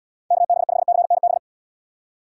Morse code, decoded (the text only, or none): KB5CNB